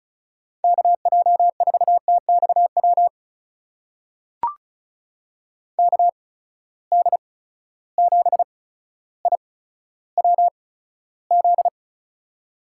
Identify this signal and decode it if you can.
Morse code, heard as a keyed tone, about 36 wpm, spaced as for 15 wpm, 700 Hz. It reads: KJ4TXW KD7IWZ
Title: USA Call Signs - 6 Characters - 8x Character Spacing - 35wpm